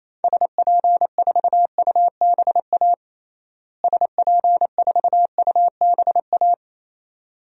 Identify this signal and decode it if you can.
Morse code, heard as a keyed tone, about 28 wpm, 700 Hz. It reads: SP4UBA SP4UBA
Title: DXCC-100 Call Signs - 6 Characters - Repeated 2x - 28wpm